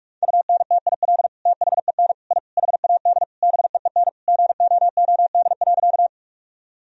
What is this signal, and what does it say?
Morse code, heard as a keyed tone, about 41 wpm, 700 Hz. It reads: UNTIL THEN I HAD BEEN GOOD.
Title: Sentences from Top 1000 Words - Review Part 1 - 45wpm